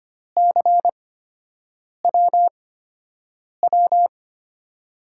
Morse code, heard as a keyed tone, about 25 wpm, spaced as for 7 wpm, 700 Hz. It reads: CWW
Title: USA Call Signs - 5 Characters - 8x Character Spacing - 25wpm